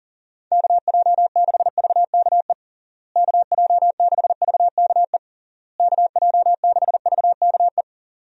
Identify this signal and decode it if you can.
Morse code, heard as a keyed tone, about 40 wpm, 700 Hz. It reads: KJ6VKE KJ6VKE KJ6VKE
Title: USA Call Signs - 6 Characters - Repeated 3x - 40wpm